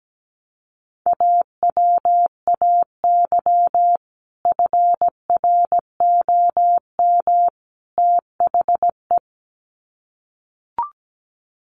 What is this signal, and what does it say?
Morse code, heard as a keyed tone, about 17 wpm, 700 Hz. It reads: AWAY FROM THE
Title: Sets of 3 Words 17wpm